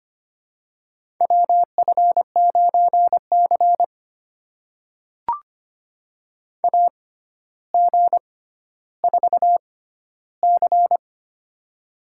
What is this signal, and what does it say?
Morse code, heard as a keyed tone, about 25 wpm, spaced as for 13 wpm, 700 Hz. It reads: WF9C AG4C
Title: USA Call Signs - 4 Characters - 6x Character Spacing - 25wpm